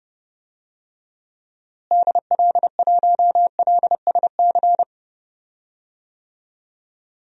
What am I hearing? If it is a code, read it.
Morse code, heard as a keyed tone, about 32 wpm, 700 Hz. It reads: DL1LSC
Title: DXCC-100 Call Signs - 6 Characters - Repeated 2x - 30wpm